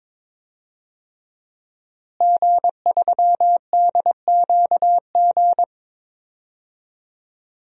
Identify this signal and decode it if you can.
Morse code, heard as a keyed tone, about 22 wpm, 700 Hz. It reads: G3DQG
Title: DXCC-100 Call Signs - 5 Characters - Repeated 3x - 22wpm